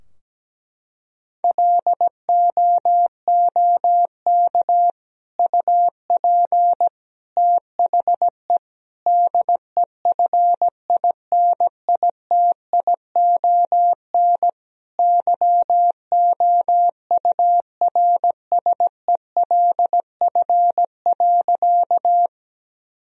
Morse code, heard as a keyed tone, about 17 wpm, 700 Hz. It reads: LOOK UP THE DEFINITION YOURSELF.
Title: Sentences from Top 1000 Words - Review Part 4 - 17wpm